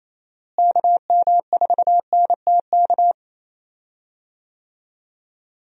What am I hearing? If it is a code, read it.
Morse code, heard as a keyed tone, about 28 wpm, 700 Hz. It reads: KM4NTK
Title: USA Call Signs - 6 Characters - Repeated 2x - 28wpm